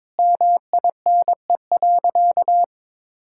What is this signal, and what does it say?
Morse code, heard as a keyed tone, about 22 wpm, 700 Hz. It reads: MINE.
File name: Sentences with 7-Word Phrases - 22wpm